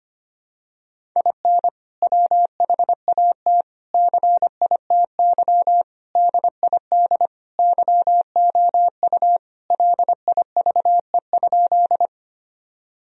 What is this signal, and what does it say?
Morse code, heard as a keyed tone, about 25 wpm, 700 Hz. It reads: IN WHAT CITY DID YOU LIVE?
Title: Sentences from Top 1000 Words - Review Part 1 - 25wpm